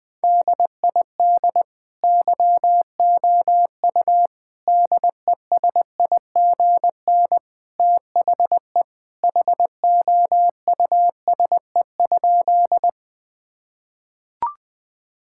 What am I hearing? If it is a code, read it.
Morse code, heard as a keyed tone, about 21 wpm, 700 Hz. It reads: DID YOU DESIGN THE HOUSE?
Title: Sentences from Top 1000 Words - Review Part 5 - 20wpm